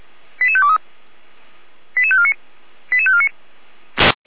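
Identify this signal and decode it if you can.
Five-tone selective call (ZVEI), heard as quick runs of five tones, tones of about 70 ms. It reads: 88523 88549 88549
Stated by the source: ZVEI five-tone selective call received on SDR.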